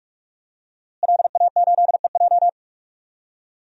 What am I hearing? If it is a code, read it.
Morse code, heard as a keyed tone, about 43 wpm, 700 Hz. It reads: LA8EJ
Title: DXCC-100 Call Signs - 5 Characters - Repeated 2x - 45wpm